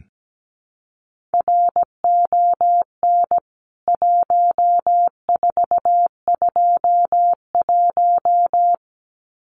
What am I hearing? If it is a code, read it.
Morse code, heard as a keyed tone, about 17 wpm, 700 Hz. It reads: RON 1421
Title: CWT Exchange 17wpm